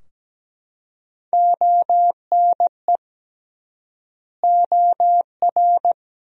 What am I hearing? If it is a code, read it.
Morse code, heard as a keyed tone, about 17 wpm, 700 Hz. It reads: ONE OR
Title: 3x Word-Spacing - Sentences from Top 100 Words - 17wpm